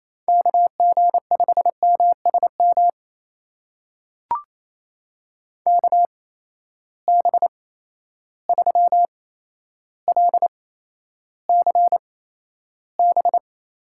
Morse code, heard as a keyed tone, about 28 wpm, spaced as for 12 wpm, 700 Hz. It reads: KG5MSM KB3LCB
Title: USA Call Signs - 6 Characters - 8x Character Spacing - 28wpm